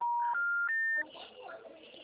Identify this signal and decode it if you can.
Special information tone, with speech.